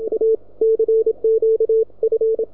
Morse code, heard as a keyed tone, about 27 wpm, 440 Hz, with radio noise. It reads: A CQ F